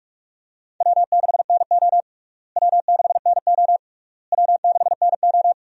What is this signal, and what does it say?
Morse code, heard as a keyed tone, about 43 wpm, 700 Hz. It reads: W6NO W6NO W6NO
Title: USA Call Signs - 4 Characters - Repeated 3x - 45wpm